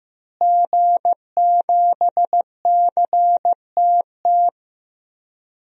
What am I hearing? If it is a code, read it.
Morse code, heard as a keyed tone, about 15 wpm, 700 Hz. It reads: G7CTT
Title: DXCC-100 Call Signs - 5 Characters - Repeated 3x - 15wpm